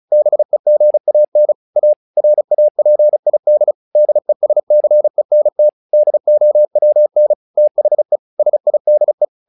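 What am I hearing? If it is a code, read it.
Morse code, heard as a keyed tone, about 35 wpm, 600 Hz. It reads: BEGAN A RAPID DESCENT DOWN THE SIDE
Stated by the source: Morse practice text "Tower Trouble" sent at 35 WPM, from the SKCC - Straight Key Century Club learning center audio files.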